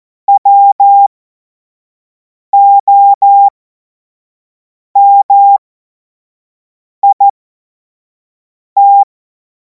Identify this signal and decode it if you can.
Morse code, heard as a keyed tone, about 14 wpm, spaced as for 5 wpm, 800 Hz. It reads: WOMIT